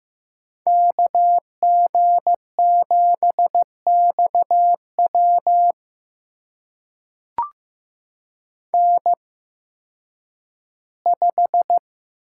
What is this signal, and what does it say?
Morse code, heard as a keyed tone, about 15 wpm, spaced as for 10 wpm, 700 Hz. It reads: KG7XW N5
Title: USA Call Signs - 5 Characters - 8x Character Spacing - 15wpm